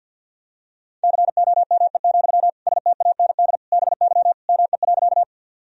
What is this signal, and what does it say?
Morse code, heard as a keyed tone, about 52 wpm, 700 Hz. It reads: COME, STAND BY ME.
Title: Sentences from Top 1000 Words - Review Part 1 - 50wpm